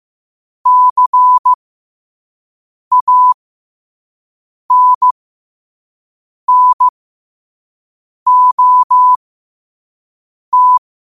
Morse code, heard as a keyed tone, about 15 wpm, spaced as for 5 wpm, 1000 Hz. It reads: CANNOT
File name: qotdfiles / 5wpm